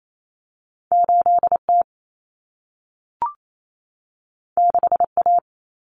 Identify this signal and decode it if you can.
Morse code, heard as a keyed tone, about 28 wpm, 700 Hz. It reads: 8T 6A
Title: Common Cut Numbers - 2-Digit 28wpm